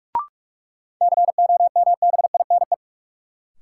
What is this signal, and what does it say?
Morse code, heard as a keyed tone, about 42 wpm, 700 Hz. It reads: COMBINE
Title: Sentences from Top 1600 Words - New Word Drill - 45wpm